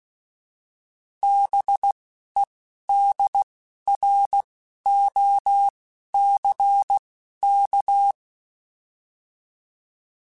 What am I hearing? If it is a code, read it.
Morse code, heard as a keyed tone, about 16 wpm, spaced as for 11 wpm, 800 Hz. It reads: BEDROCK
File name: fromSMs / FromSMs mp3s / Cities@10